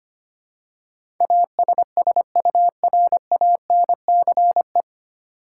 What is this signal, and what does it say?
Morse code, heard as a keyed tone, about 25 wpm, 700 Hz. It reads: ASSURANCE
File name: practice / 25wpm / 9 Letter Words - First 3 Letters Given - 25wpm